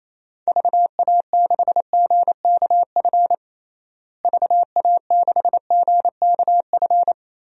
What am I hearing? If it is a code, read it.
Morse code, heard as a keyed tone, about 28 wpm, 700 Hz. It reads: VA6GKF VA6GKF
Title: DXCC-100 Call Signs - 6 Characters - Repeated 3x - 28wpm